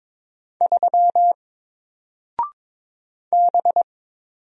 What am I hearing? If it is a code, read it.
Morse code, heard as a keyed tone, about 22 wpm, 700 Hz. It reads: 3 B
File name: Single-Letter-Number-22wpm